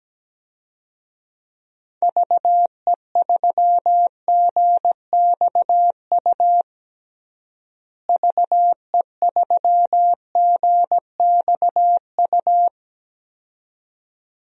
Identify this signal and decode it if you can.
Morse code, heard as a keyed tone, about 17 wpm, 700 Hz. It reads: VE3GXU VE3GXU
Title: DXCC-100 Call Signs - 6 Characters - Repeated 2x - 17wpm